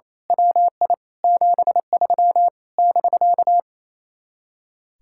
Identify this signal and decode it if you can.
Morse code, heard as a keyed tone, about 26 wpm, 700 Hz. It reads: WI 73 <BK>
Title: Speed-Racing Common QSO Phrases Encore - 28wpm